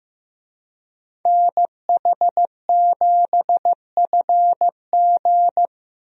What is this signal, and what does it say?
Morse code, heard as a keyed tone, about 15 wpm, 700 Hz. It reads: NH7FG
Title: USA Call Signs - 5 Characters - Repeated 2x - 15wpm